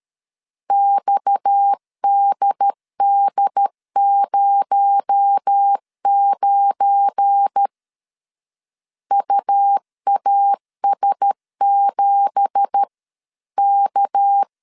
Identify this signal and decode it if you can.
Morse code, heard as a keyed tone, about 13 wpm, 800 Hz. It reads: XDD09 UAS7K
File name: CW / DET / 1 9 1500C S B 12wpm 3819